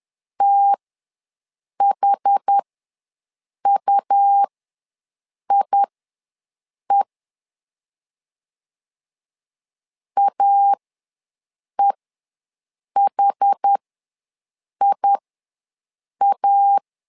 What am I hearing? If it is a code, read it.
Morse code, heard as a keyed tone, about 10 wpm, spaced as for 6 wpm, 800 Hz. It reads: THUIE AEHIA
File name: LEZIONI / 2 UHDN / 2 UH ws 10wpm 2029